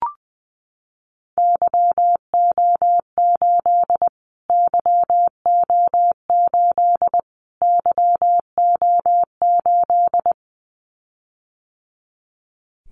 Morse code, heard as a keyed tone, about 20 wpm, 700 Hz. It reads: YO8 YO8 YO8
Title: Callsign Prefixes for Top 25 DXCC - Repeated 3x - 20wpm